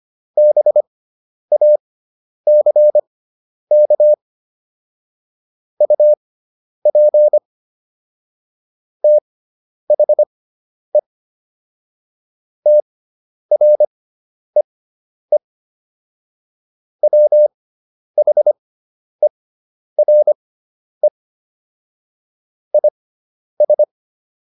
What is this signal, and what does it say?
Morse code, heard as a keyed tone, about 25 wpm, spaced as for 10 wpm, 600 Hz. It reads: BACK UP THE TREE WHERE IS